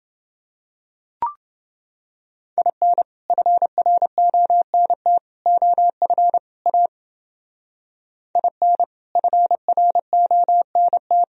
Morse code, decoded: IN FRONT OF A IN FRONT

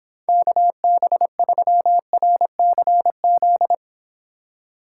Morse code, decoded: KB3RCZ

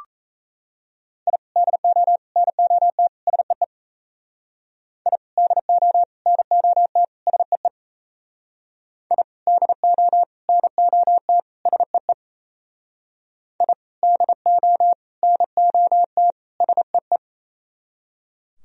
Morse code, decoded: I DO NOT SEE I DO NOT SEE I DO NOT SEE I DO NOT SEE

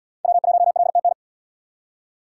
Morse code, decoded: W1LTN

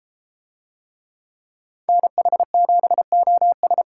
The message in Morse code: NH7OS

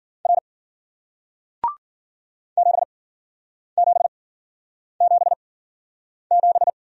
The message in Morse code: R 7777

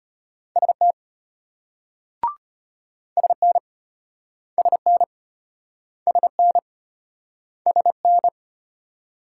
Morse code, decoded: ST SN SN SN SN

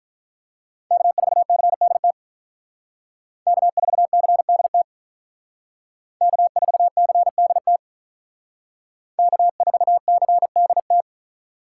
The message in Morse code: K4CDT K4CDT K4CDT K4CDT

